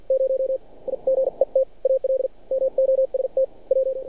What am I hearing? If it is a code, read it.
Morse code, heard as a keyed tone, about 47 wpm, spaced as for 38 wpm, 530 Hz. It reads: 0 I GET ALMOST W
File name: archived / CRSnet